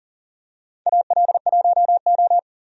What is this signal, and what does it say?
Morse code, heard as a keyed tone, about 40 wpm, 700 Hz. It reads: AL1O
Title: USA Call Signs - 4 Characters - 40wpm